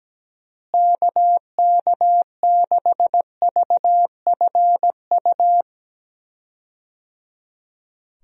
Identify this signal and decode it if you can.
Morse code, heard as a keyed tone, about 17 wpm, 700 Hz. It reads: KK6VFU
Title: USA Call Signs - 6 Characters - Repeated 3x - 17wpm